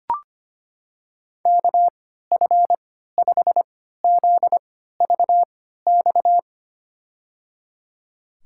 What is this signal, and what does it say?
Morse code, heard as a keyed tone, about 25 wpm, spaced as for 14 wpm, 700 Hz. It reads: KF5ZVX
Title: USA Call Signs - 6 Characters - 3x Character Spacing - 25wpm